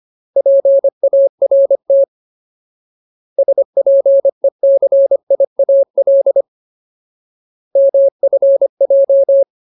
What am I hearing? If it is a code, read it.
Morse code, heard as a keyed tone, about 25 wpm, 550 Hz. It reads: PART SPECIAL MFJ
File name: Ultimate Word List / 25WPM Ultimate